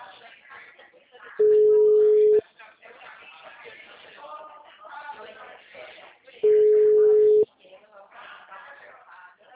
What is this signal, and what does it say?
Ringback tone (Europe, 425 Hz), with speech.
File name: out- -2010